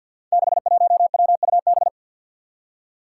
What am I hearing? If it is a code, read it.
Morse code, heard as a keyed tone, about 49 wpm, 700 Hz. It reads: 61WUB
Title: Five Challenging Letter-Number Combinations 50wpm